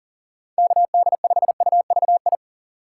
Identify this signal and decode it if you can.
Morse code, heard as a keyed tone, about 40 wpm, 700 Hz. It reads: KD5UVI
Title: USA Call Signs - 6 Characters - 3x Character Spacing - 40wpm